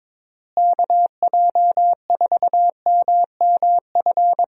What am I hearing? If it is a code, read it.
Morse code, heard as a keyed tone, about 22 wpm, 700 Hz. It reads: KJ4MMF
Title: USA Call Signs - 6 Characters - Repeated 3x - 22wpm